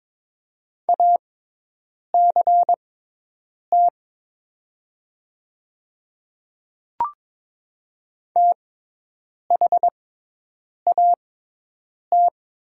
Morse code, heard as a keyed tone, about 22 wpm, spaced as for 8 wpm, 700 Hz. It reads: ACT THAT
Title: 6x Character Spacing - Top 300 Words - 22wpm